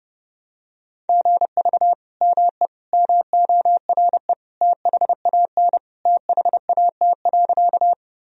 Morse code, heard as a keyed tone, about 30 wpm, 700 Hz. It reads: GV ME MORE THAN THAT.
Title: Sentences from Top 200 Words and Abbreviations 30wpm